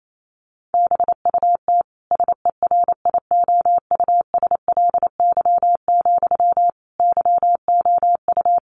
Morse code, heard as a keyed tone, about 28 wpm, 700 Hz. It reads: BUT SERIOUSLY, YOU
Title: Sentences from Top 1200 Words 28wpm